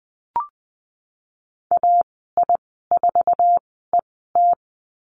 Morse code, heard as a keyed tone, about 20 wpm, spaced as for 15 wpm, 700 Hz. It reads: AI4ET
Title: 048 Lesson 9g - Number 4 - Call Signs 20wpm